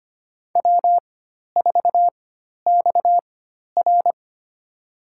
Morse code, heard as a keyed tone, about 25 wpm, spaced as for 12 wpm, 700 Hz. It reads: W4XR
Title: USA Call Signs - 4x Character Spacing - 25wpm